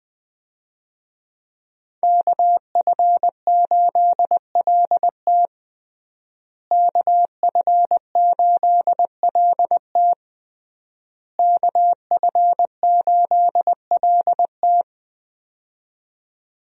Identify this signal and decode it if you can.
Morse code, heard as a keyed tone, about 20 wpm, 700 Hz. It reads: KF8LT KF8LT KF8LT
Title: USA Call Signs - 5 Characters - Repeated 3x - 20wpm